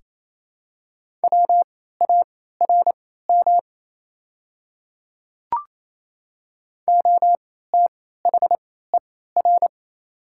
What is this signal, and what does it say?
Morse code, heard as a keyed tone, about 28 wpm, spaced as for 16 wpm, 690 Hz. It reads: WARM OTHER
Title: 3x Character Spacing - Top 500 Words - 28wpm